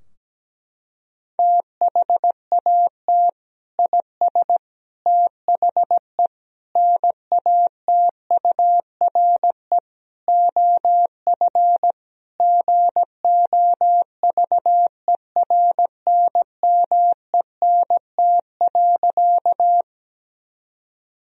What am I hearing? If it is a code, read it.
Morse code, heard as a keyed tone, about 17 wpm, 700 Hz. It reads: THAT IS THE NATURE OF GOVERNMENT.
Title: Sentences from Top 1000 Words - Review Part 3 - 17wpm